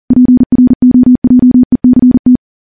Morse code, heard as a keyed tone, about 40 wpm, 250 Hz. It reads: PROJECT
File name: morse / morse-training-40WPM